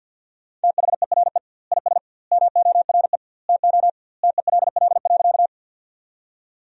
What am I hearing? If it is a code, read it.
Morse code, heard as a keyed tone, about 52 wpm, 700 Hz. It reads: THERE IS MORE TO TELL.
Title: 2x Word-Spacing - Sentences from Top 200 Words - 50wpm